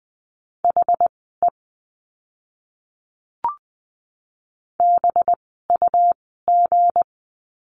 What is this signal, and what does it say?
Morse code, heard as a keyed tone, about 20 wpm, spaced as for 14 wpm, 700 Hz. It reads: HE BUG
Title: 185 Lesson 30f - Letter b - Words 20wpm